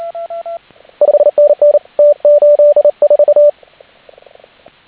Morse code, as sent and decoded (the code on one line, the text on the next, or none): ..... -. -. / - ---.. ....-
5NN T84